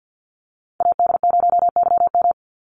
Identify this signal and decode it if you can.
Morse code, heard as a keyed tone, about 50 wpm, 700 Hz. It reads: AD0YM
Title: NA SOTA Chaser Callsigns 50wpm